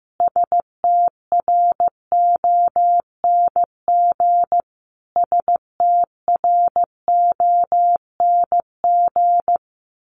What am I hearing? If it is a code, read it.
Morse code, heard as a keyed tone, about 15 wpm, 700 Hz. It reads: STRONG STRONG
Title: Top 500 Words Repeated 15wpm